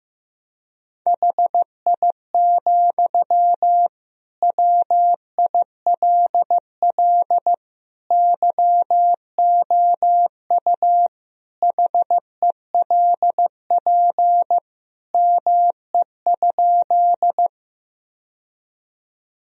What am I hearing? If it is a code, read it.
Morse code, heard as a keyed tone, about 15 wpm, 700 Hz. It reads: HI, WILL YOU HELP ME?
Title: Sentences from Top 1000 Words - Review Part 3 - 15wpm